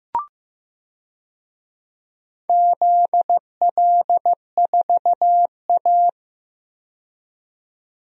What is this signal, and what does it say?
Morse code, heard as a keyed tone, about 15 wpm, 700 Hz. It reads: ZL4A
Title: DXCC-100 Call Signs - 4 Characters - 15wpm